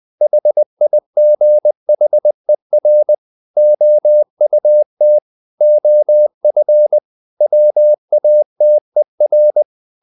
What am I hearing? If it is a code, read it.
Morse code, heard as a keyed tone, about 20 wpm, 600 Hz. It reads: HIGHER OUT OF WATER
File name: the old man and the sea part 2 20wpm